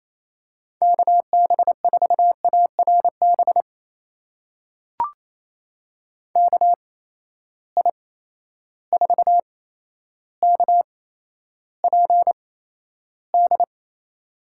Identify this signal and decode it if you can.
Morse code, heard as a keyed tone, about 28 wpm, spaced as for 12 wpm, 700 Hz. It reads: KB4ARB KI4KPD
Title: USA Call Signs - 8x Character Spacing - 28wpm